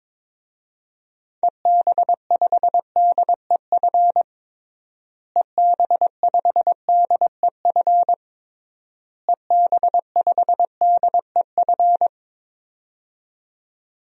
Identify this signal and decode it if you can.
Morse code, heard as a keyed tone, about 22 wpm, 700 Hz. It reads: EB5DEF EB5DEF EB5DEF
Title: DXCC-100 Call Signs - 6 Characters - Repeated 3x - 22wpm